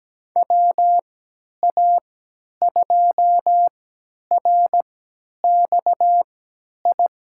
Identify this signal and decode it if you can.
Morse code, heard as a keyed tone, about 17 wpm, spaced as for 10 wpm, 700 Hz. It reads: WA2RXI
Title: USA Call Signs - 6 Characters - 3x Character Spacing - 17wpm